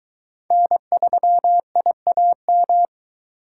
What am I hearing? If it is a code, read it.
Morse code, heard as a keyed tone, about 22 wpm, 700 Hz. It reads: N3IAM